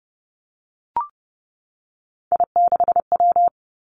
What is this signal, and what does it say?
Morse code, heard as a keyed tone, about 31 wpm, 700 Hz. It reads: I6W
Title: Three Challenging Letter-Number Combinations 30wpm